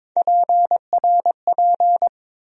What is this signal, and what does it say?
Morse code, heard as a keyed tone, about 22 wpm, 700 Hz. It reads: PRP
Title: USA Call Sign Postfixes - 3 Letters 22wpm